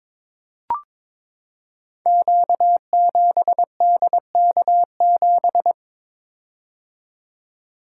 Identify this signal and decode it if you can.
Morse code, heard as a keyed tone, about 22 wpm, 700 Hz. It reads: Q7DK7
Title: Code Groups - 5 Letter:Numbers - 22wpm